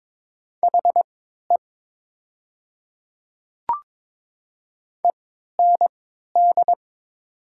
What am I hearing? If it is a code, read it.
Morse code, heard as a keyed tone, about 22 wpm, spaced as for 12 wpm, 690 Hz. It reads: HE END
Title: 3x Character Spacing - Top 500 Words - 22wpm